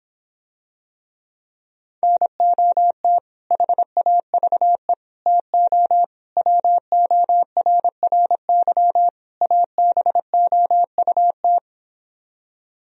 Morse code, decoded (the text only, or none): NOT HAVE TO WORRY ABOUT